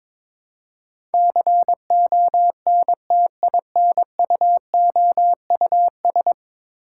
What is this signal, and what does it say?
Morse code, heard as a keyed tone, about 22 wpm, 700 Hz. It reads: CONTINUOUS
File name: practice / 22wpm / 10 Letter Words - First 3 Letters Given - 22wpm